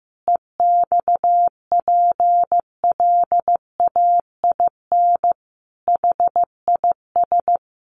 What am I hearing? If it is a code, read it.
Morse code, heard as a keyed tone, about 15 wpm, 700 Hz. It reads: EXPLAIN HIS
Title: Sentences from Top 1100 Words 15wpm